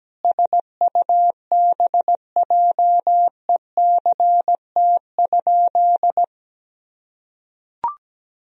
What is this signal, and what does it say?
Morse code, heard as a keyed tone, about 17 wpm, 700 Hz. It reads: SUBJECT?
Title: Sentences from Top 1000 Words - Review Part 1 - 17wpm